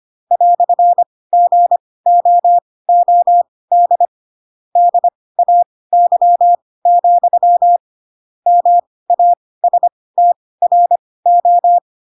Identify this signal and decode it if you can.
Morse code, heard as a keyed tone, about 25 wpm, spaced as for 18 wpm, 700 Hz. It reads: "GOOD DAY, MASTRO
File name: Pinocchio / Pinocchio ch2 12 25 ik1hge